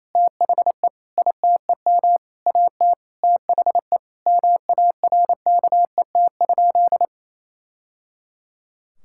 Morse code, heard as a keyed tone, about 28 wpm, 700 Hz. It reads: THE ITEM AT THE MARKET?
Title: Sentences from Top 1000 Words - Review Part 6 - 28wpm